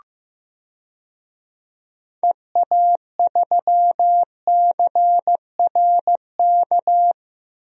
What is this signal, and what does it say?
Morse code, heard as a keyed tone, about 15 wpm, 700 Hz. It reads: EA3CRK
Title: DXCC-100 Call Signs - 6 Characters - Repeated 2x - 15wpm